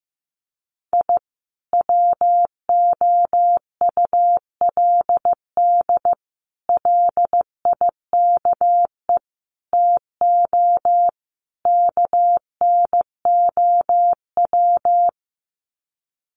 Morse code, heard as a keyed tone, about 15 wpm, 700 Hz. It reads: I WOULD LIKE TO KNOW
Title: Sets of 5 Words Encore 15wpm